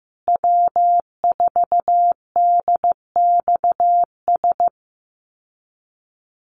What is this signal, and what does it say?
Morse code, heard as a keyed tone, about 15 wpm, 700 Hz. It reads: W4DXS
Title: USA Call Signs Repeated 2x 15wpm